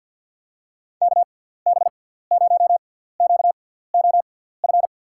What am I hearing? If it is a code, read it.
Morse code, heard as a keyed tone, about 50 wpm, spaced as for 17 wpm, 700 Hz. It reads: KB0QOF
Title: USA Call Signs - 6x Character Spacing - 50wpm